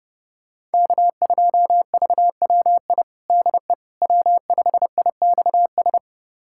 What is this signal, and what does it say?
Morse code, heard as a keyed tone, about 32 wpm, 700 Hz. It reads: K2VWI DE W5IXS
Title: USA Callsign DE Callsign - 5 Characters - 30wpm